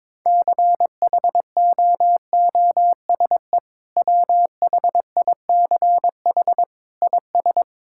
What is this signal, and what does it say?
Morse code, heard as a keyed tone, about 22 wpm, 700 Hz. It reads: CHOOSE WHICH IS
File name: practice / 22wpm / Sentences with 2-Word Phrases - 22wpm